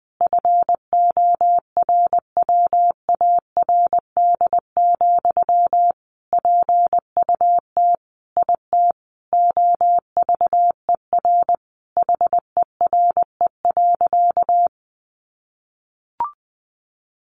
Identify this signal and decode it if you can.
Morse code, heard as a keyed tone, about 20 wpm, 700 Hz. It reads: FORWARD, PUT IT OVER HERE.
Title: Sentences from Top 1500 Words 20wpm